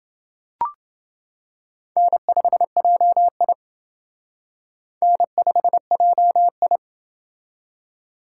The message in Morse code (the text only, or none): N5JI N5JI